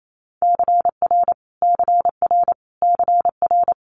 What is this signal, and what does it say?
Morse code, heard as a keyed tone, about 28 wpm, 700 Hz. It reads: CR CR CR
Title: Multi-letter Phonemes Repeated 3x 28wpm